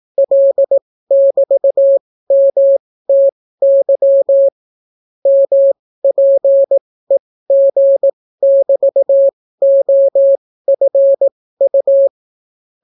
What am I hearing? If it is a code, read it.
Morse code, heard as a keyed tone, about 18 wpm, spaced as for 15 wpm, 550 Hz. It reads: L<BT>MTY MPEG<BT>OFU